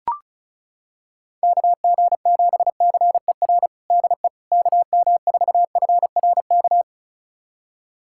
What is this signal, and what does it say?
Morse code, heard as a keyed tone, about 34 wpm, 700 Hz. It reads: KG7CER DE KM4FRK
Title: USA Callsign DE Callsign 35wpm